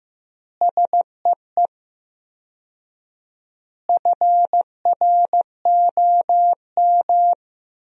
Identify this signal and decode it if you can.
Morse code, heard as a keyed tone, about 15 wpm, 700 Hz. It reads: SEE FROM
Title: 4x Word-Spacing - Sentences from Top 100 Words - 15wpm